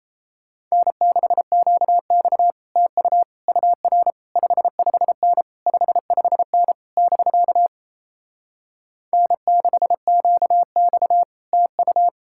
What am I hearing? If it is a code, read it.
Morse code, heard as a keyed tone, about 37 wpm, 700 Hz. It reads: N6QX TU UR 55N 55N <BK> N6QX TU